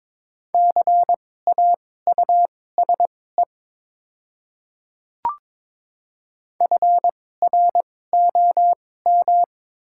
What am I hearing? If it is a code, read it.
Morse code, heard as a keyed tone, about 22 wpm, spaced as for 16 wpm, 700 Hz. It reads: CAUSE FROM
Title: 2x Character Spacing - Top 200 Words - 22wpm